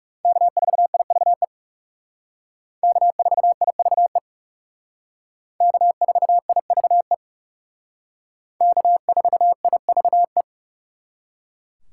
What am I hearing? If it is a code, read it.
Morse code, heard as a keyed tone, about 36 wpm, 700 Hz. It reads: K4IVE K4IVE K4IVE K4IVE
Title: Speed Racing - USA 1x3 Call Signs 30wpm